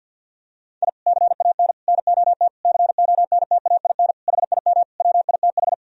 Morse code, heard as a keyed tone, about 51 wpm, 700 Hz. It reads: I CAN NOT CONTAIN HIM WITH
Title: Sentences from Top 1000 Words - Review Part 3 - 50wpm